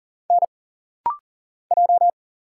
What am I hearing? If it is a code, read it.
Morse code, heard as a keyed tone, about 40 wpm, 700 Hz. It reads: N J
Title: Single Letter-Number Alt - Rapid-Fire 40wpm